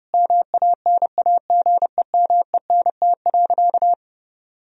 Morse code, heard as a keyed tone, about 30 wpm, 700 Hz. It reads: MANAGEMENT.
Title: Sentences from Top 1000 Words - Review Part 4 - 30wpm